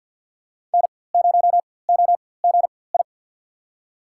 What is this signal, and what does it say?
Morse code, heard as a keyed tone, about 48 wpm, spaced as for 23 wpm, 700 Hz. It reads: N0OGI
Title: USA Call Signs - 5 Characters - 4x Character Spacing - 50wpm